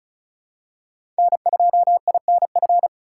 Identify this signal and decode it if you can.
Morse code, heard as a keyed tone, about 34 wpm, 700 Hz. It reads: N2INF
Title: USA Call Signs - 5 Characters - 4x Character Spacing - 35wpm